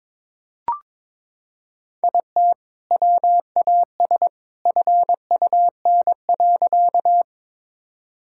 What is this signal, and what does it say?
Morse code, heard as a keyed tone, about 21 wpm, 700 Hz. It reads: IT WAS FUN.
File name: practice / 22wpm / Sentences with 2-Word Phrases - 22wpm